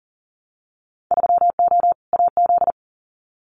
Morse code, heard as a keyed tone, about 40 wpm, 700 Hz. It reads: 3O AZ
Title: Winter Field Day Exchanges 40wpm